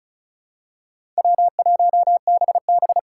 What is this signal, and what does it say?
Morse code, heard as a keyed tone, about 34 wpm, 700 Hz. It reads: W1BB
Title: USA Call Signs - 4 Characters - 2x Character Spacing - 35wpm